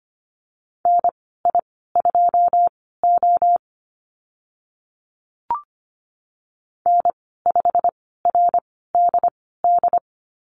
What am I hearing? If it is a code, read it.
Morse code, heard as a keyed tone, about 25 wpm, spaced as for 16 wpm, 700 Hz. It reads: NI2O N5RDD
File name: Lesson 15g - Number 5 - Call Signs 25wpm